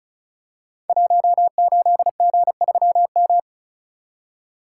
Morse code, heard as a keyed tone, about 36 wpm, 700 Hz. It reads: 18G3M
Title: Code Groups - 5 Letter:Numbers - 35wpm